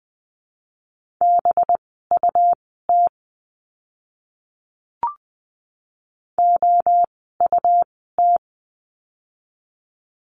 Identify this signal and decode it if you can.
Morse code, heard as a keyed tone, about 19 wpm, spaced as for 14 wpm, 700 Hz. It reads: BUT OUT
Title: 206 Lesson 33f - Letter j - Words 20wpm